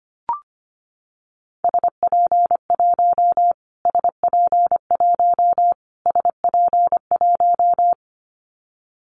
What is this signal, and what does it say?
Morse code, heard as a keyed tone, about 25 wpm, 700 Hz. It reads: SP1 SP1 SP1
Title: Callsign Prefixes for Top 25 DXCC - Repeated 3x - 25wpm